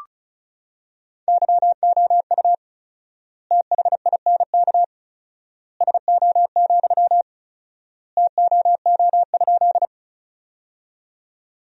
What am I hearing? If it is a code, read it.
Morse code, heard as a keyed tone, about 36 wpm, 700 Hz. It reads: YOU THINK SO, TOO?
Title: 4x Word-Spacing - Sentences from Top 200 Words - 35wpm